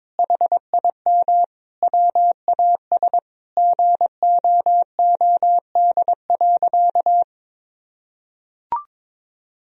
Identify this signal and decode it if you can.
Morse code, heard as a keyed tone, about 22 wpm, 700 Hz. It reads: HIM WAS GOOD.
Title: Sentences from Top 1000 Words - Review Part 4 - 22wpm